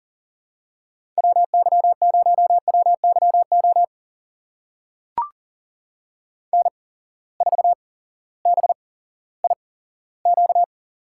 Morse code, heard as a keyed tone, about 40 wpm, spaced as for 18 wpm, 700 Hz. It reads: WY0WYO N4BIQ
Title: USA Call Signs - 8x Character Spacing - 40wpm